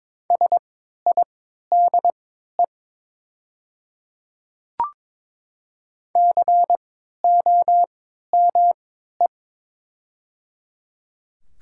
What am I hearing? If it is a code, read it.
Morse code, heard as a keyed tone, about 22 wpm, spaced as for 12 wpm, 700 Hz. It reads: SIDE COME
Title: 3x Character Spacing - Top 100 Words - 22wpm